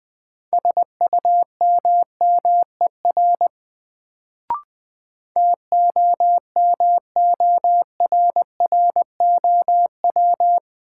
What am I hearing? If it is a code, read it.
Morse code, heard as a keyed tone, about 20 wpm, 700 Hz. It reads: SUMMER TOMORROW
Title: Sentences from Top 1100 Words - New Word Drill Alt - 20wpm